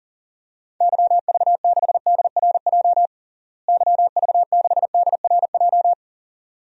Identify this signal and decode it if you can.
Morse code, heard as a keyed tone, about 40 wpm, 700 Hz. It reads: YV6DRJ YV6DRJ
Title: DXCC-100 Call Signs - 6 Characters - Repeated 2x - 40wpm